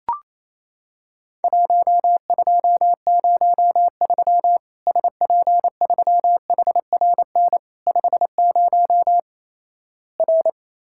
Morse code, heard as a keyed tone, about 28 wpm, 700 Hz. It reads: 1203 SP3HRN 50 R
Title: Worked All Europe DX Contest - QTC Exchange - 28wpm